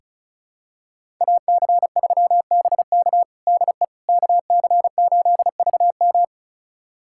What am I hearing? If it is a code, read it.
Morse code, heard as a keyed tone, about 34 wpm, 700 Hz. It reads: AC3BK DE KC8VM
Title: USA Callsign DE Callsign - 5 Characters - 35wpm